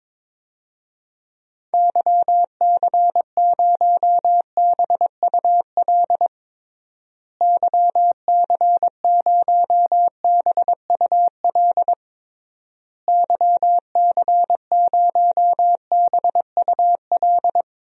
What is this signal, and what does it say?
Morse code, heard as a keyed tone, about 22 wpm, 700 Hz. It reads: YC0BUL YC0BUL YC0BUL
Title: DXCC-100 Call Signs - 6 Characters - Repeated 3x - 22wpm